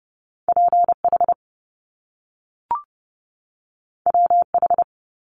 Morse code, heard as a keyed tone, about 30 wpm, 700 Hz. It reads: P H W H
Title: Top Two Letter Combinations in Words 30wpm